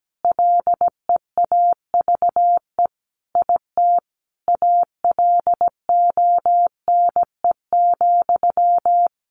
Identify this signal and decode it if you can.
Morse code, heard as a keyed tone, about 17 wpm, 700 Hz. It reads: LEAVE IT ALONE,
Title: Sentences from Top 1000 Words 17wpm